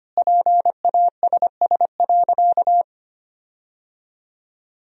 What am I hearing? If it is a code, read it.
Morse code, heard as a keyed tone, about 25 wpm, 700 Hz. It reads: PASS.
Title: 4x Word-Spacing - Sentences from Top 400 Words - 25wpm